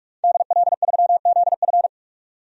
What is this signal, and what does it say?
Morse code, heard as a keyed tone, about 46 wpm, 700 Hz. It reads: DL3ZF
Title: DXCC-100 Call Signs - 5 Characters - 45wpm